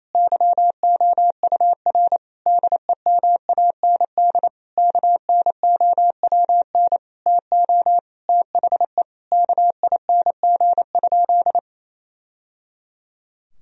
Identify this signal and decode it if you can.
Morse code, heard as a keyed tone, about 28 wpm, 700 Hz. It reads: YOUR DEMAND KNOWN TO THE KING?
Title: Sentences from Top 1000 Words - Review Part 4 - 28wpm